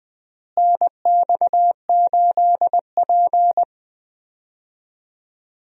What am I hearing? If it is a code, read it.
Morse code, heard as a keyed tone, about 20 wpm, 700 Hz. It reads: NX8P
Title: USA Call Signs - 4 Characters - Repeated 2x - 20wpm